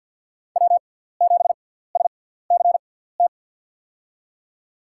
Morse code, heard as a keyed tone, about 49 wpm, spaced as for 17 wpm, 700 Hz. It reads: W7SCT
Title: USA Call Signs - 5 Characters - 6x Character Spacing - 50wpm